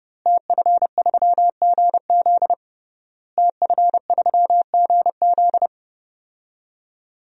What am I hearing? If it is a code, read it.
Morse code, heard as a keyed tone, about 30 wpm, 700 Hz. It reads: TF3GZ TF3GZ
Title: DXCC-100 Call Signs - 5 Characters - Repeated 3x - 30wpm